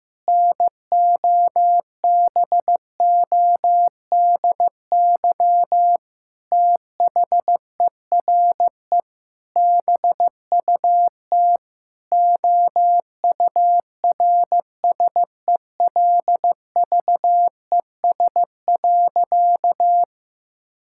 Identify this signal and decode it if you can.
Morse code, heard as a keyed tone, about 15 wpm, 700 Hz. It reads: NOBODY THERE BUT OURSELVES.
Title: Sentences from Top 1600 Words 15wpm